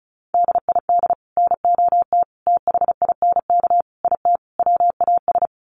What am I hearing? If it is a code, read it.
Morse code, heard as a keyed tone, about 34 wpm, 700 Hz. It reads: DID NOT THINK IT WAS
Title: Sets of 5 Words Encore 35wpm